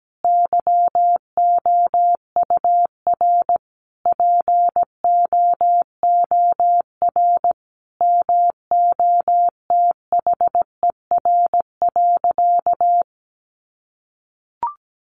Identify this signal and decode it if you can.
Morse code, heard as a keyed tone, about 17 wpm, 700 Hz. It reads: YOUR POOR MOTHER.
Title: Sentences from Top 1000 Words 17wpm